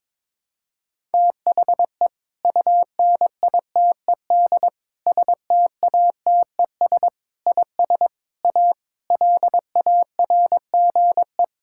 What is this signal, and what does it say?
Morse code, heard as a keyed tone, about 22 wpm, 700 Hz. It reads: THE UNITED STATES IS A LARGE
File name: practice / 22wpm / Sentences with 4-Word Phrases - 22wpm